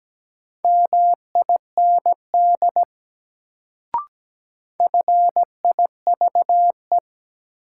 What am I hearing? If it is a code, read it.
Morse code, heard as a keyed tone, about 17 wpm, 700 Hz. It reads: MIND FIVE
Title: Sets of 5 Words - New Word Drill - 17wpm